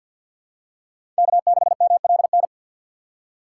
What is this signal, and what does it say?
Morse code, heard as a keyed tone, about 46 wpm, 700 Hz. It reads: K6MLN
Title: USA Call Signs - 5 Characters - 2x Character Spacing - 50wpm